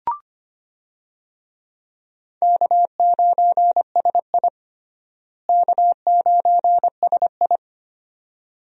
Morse code, heard as a keyed tone, about 25 wpm, 700 Hz. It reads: K9SI K9SI
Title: USA Call Signs - 4 Characters - Repeated 2x - 25wpm